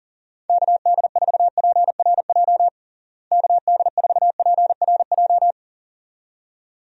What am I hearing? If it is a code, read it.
Morse code, heard as a keyed tone, about 37 wpm, 700 Hz. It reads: KD4PRJ KD4PRJ
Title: USA Call Signs - 6 Characters - Repeated 2x - 40wpm